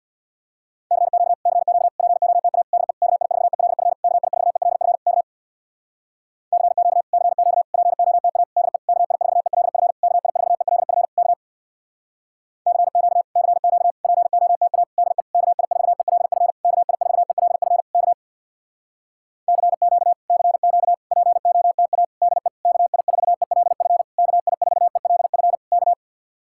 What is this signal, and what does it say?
Morse code, heard as a keyed tone, about 64 wpm, 700 Hz. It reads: CQ CQ POTA DE KI4ELF KI4ELF K CQ CQ POTA DE KI4ELF KI4ELF K CQ CQ POTA DE KI4ELF KI4ELF K CQ CQ POTA DE KI4ELF KI4ELF K